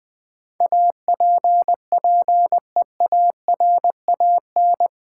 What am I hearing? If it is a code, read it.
Morse code, heard as a keyed tone, about 20 wpm, 700 Hz. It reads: APPEARAN
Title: Build A Word - 4 Letters At A Time - 10-Letter Words - 20wpm